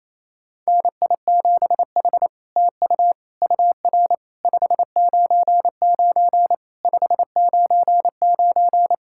Morse code, heard as a keyed tone, about 28 wpm, 700 Hz. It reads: NI7H TU UR 599 599
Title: SOTA QSO Practice 28wpm